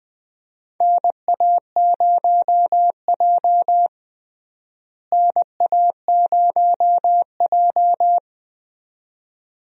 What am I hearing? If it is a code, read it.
Morse code, heard as a keyed tone, about 21 wpm, 700 Hz. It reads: NA0J NA0J
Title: USA Call Signs - 4 Characters - Repeated 2x - 20wpm